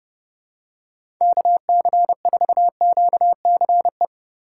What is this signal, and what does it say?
Morse code, heard as a keyed tone, about 30 wpm, 700 Hz. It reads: KC4QCE
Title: USA Call Signs - 6 Characters - 2x Character Spacing - 30wpm